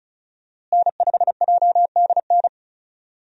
Morse code, heard as a keyed tone, about 34 wpm, 700 Hz. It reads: N5JDN
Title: USA Call Signs - 5 Characters - Repeated 3x - 35wpm